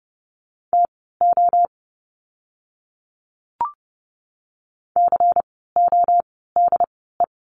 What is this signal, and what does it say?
Morse code, heard as a keyed tone, about 30 wpm, spaced as for 17 wpm, 700 Hz. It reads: TO CODE
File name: Lesson 16f - Letter c - Words 30wpm